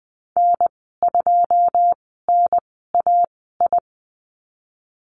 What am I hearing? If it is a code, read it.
Morse code, heard as a keyed tone, about 20 wpm, spaced as for 15 wpm, 700 Hz. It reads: N2NAI
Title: 088 Lesson 15g - Number 5 - Call Signs 20wpm